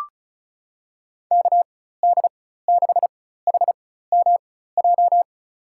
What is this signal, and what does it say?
Morse code, heard as a keyed tone, about 34 wpm, spaced as for 16 wpm, 700 Hz. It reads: KD6HMJ
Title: USA Call Signs - 6 Characters - 4x Character Spacing - 35wpm